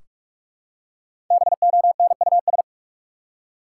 Morse code, heard as a keyed tone, about 46 wpm, 700 Hz. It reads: BONUS
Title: Five Letter Words - 45wpm